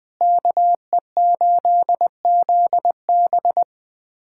Morse code, heard as a keyed tone, about 20 wpm, 700 Hz. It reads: KE8ZB
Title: USA Callsign DE Callsign - 5 Characters - 20wpm